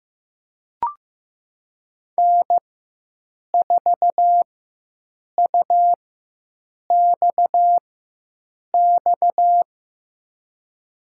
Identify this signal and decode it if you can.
Morse code, heard as a keyed tone, about 15 wpm, spaced as for 7 wpm, 700 Hz. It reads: N4UXX
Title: USA Call Signs - 5 Characters - 4x Character Spacing - 15wpm